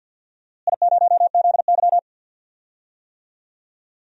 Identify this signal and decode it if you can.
Morse code, heard as a keyed tone, about 52 wpm, 700 Hz. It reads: I0ZY
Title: DXCC-100 Call Signs - 4 Characters - Repeated 2x - 50wpm